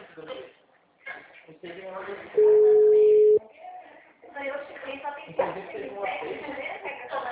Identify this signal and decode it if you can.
Ringback tone (Europe, 425 Hz), with speech.
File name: out- -2003- - - .70084